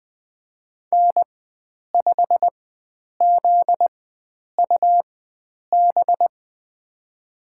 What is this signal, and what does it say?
Morse code, heard as a keyed tone, about 20 wpm, spaced as for 9 wpm, 700 Hz. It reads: N5ZUB
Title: USA Call Signs - 4x Character Spacing - 20wpm